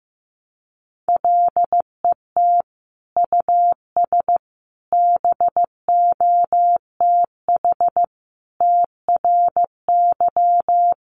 Morse code, heard as a keyed tone, about 15 wpm, 700 Hz. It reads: LET US BOTH TRY
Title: Sentences from Top 1200 Words 15wpm